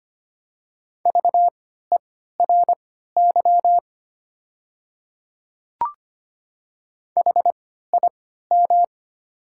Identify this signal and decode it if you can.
Morse code, heard as a keyed tone, about 25 wpm, spaced as for 14 wpm, 700 Hz. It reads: VERY HIM
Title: 3x Character Spacing - Top 200 Words - 25wpm